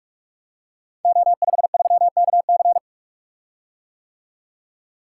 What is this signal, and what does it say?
Morse code, heard as a keyed tone, about 44 wpm, 700 Hz. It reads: O53KC
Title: Code Groups - 5 Letter:Numbers - 45wpm